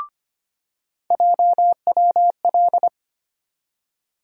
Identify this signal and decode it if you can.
Morse code, heard as a keyed tone, about 25 wpm, 700 Hz. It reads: JWL
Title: USA Call Sign Postfixes - 3 Letters 25wpm